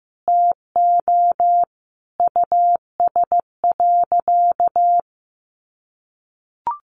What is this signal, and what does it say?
Morse code, heard as a keyed tone, about 15 wpm, 700 Hz. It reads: TO US.
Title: Sentences from Top 700 Words 15wpm